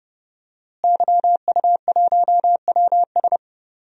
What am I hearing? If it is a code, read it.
Morse code, heard as a keyed tone, about 32 wpm, 700 Hz. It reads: YU1WS
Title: DXCC-100 Call Signs - 5 Characters - Repeated 2x - 30wpm